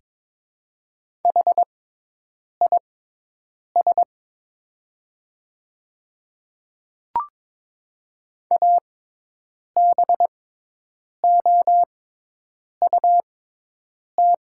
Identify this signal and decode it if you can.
Morse code, heard as a keyed tone, about 22 wpm, spaced as for 8 wpm, 700 Hz. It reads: HIS ABOUT
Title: 6x Character Spacing - Top 100 Words - 22wpm